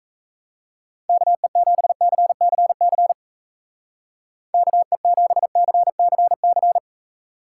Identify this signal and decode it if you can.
Morse code, heard as a keyed tone, about 33 wpm, 700 Hz. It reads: KE7CCC KE7CCC
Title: Speed Racing - USA 2x3 Call Signs 28wpm